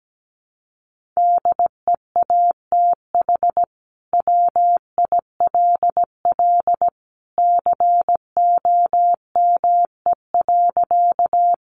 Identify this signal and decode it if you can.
Morse code, heard as a keyed tone, about 17 wpm, 700 Hz. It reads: DEATH WILL COME.
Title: Sentences from Top 1100 Words 17wpm